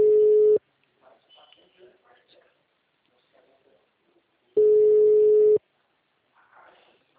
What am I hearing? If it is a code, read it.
Ringback tone (Europe, 425 Hz), with speech.
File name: out- -2010- - - .32054